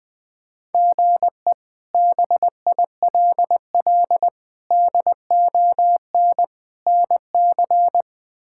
Morse code, heard as a keyed tone, about 20 wpm, 700 Hz. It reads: GE BILL DON NC
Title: K1USN SST QSO Practice - 20wpm